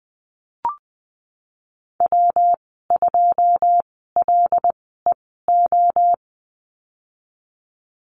Morse code, heard as a keyed tone, about 19 wpm, spaced as for 14 wpm, 700 Hz. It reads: W2LEO
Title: 140 Lesson 23g - Letter f - Call Signs 20wpm